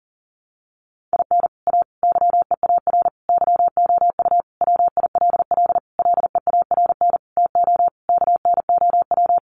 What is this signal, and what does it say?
Morse code, heard as a keyed tone, about 40 wpm, 700 Hz. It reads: IN A YEAR YOU WILL LEARN TO KNOW
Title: Sentences from Top 600 Words 40wpm